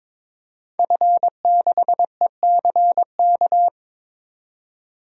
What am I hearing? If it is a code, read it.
Morse code, heard as a keyed tone, about 21 wpm, 700 Hz. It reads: F6ECK
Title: DXCC-100 Call Signs - 5 Characters - 22wpm